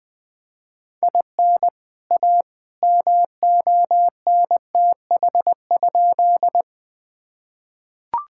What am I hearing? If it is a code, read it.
Morse code, heard as a keyed tone, about 20 wpm, 700 Hz. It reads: IN A MONTH?
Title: Sentences from Top 1000 Words - Review Part 5 - 20wpm